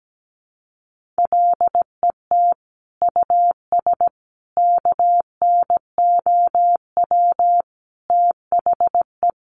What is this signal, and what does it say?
Morse code, heard as a keyed tone, about 17 wpm, 700 Hz. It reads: LET US KNOW THE
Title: Sentences from Top 1200 Words 17wpm